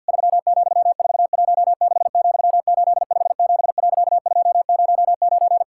Morse code, heard as a keyed tone, about 50 wpm, 700 Hz. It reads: 3,416,857.209
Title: Pangram Challenge v2 - 50wpm